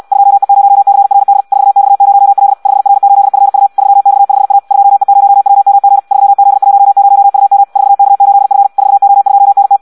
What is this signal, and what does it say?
Morse code, heard as a keyed tone, about 74 wpm, 800 Hz. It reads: 2E0CRD 4X0L 5R8FU 3X5A 2E0WMG 3Z10UM 4K8F 4O2NT